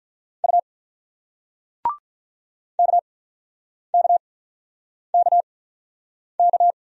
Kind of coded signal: Morse code

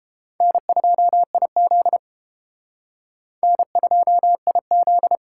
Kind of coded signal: Morse code

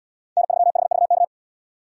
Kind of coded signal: Morse code

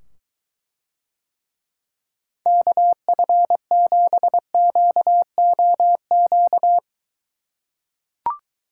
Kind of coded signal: Morse code

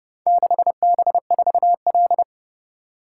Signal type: Morse code

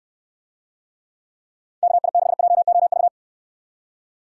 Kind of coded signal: Morse code

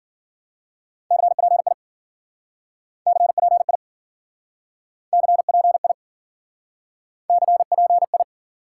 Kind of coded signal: Morse code

